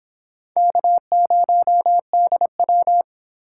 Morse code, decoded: K0DW

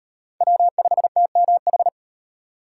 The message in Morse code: W5TMH

